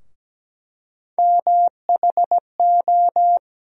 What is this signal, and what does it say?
Morse code, heard as a keyed tone, about 17 wpm, 700 Hz. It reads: MHO
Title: USA Call Sign Postfixes - 3 Letters 17wpm